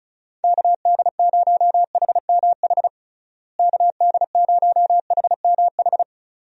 Morse code, heard as a keyed tone, about 36 wpm, 700 Hz. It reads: KD0HMH KD0HMH
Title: USA Call Signs - 6 Characters - Repeated 2x - 35wpm